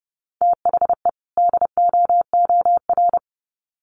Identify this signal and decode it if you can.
Morse code, heard as a keyed tone, about 30 wpm, 700 Hz. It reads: THE DOOR
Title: Sets of 2 Words 30wpm